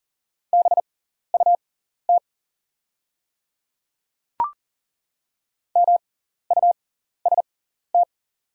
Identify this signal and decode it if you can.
Morse code, heard as a keyed tone, about 40 wpm, spaced as for 14 wpm, 690 Hz. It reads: BUT MUST
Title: 6x Character Spacing - Top 200 Words - 40wpm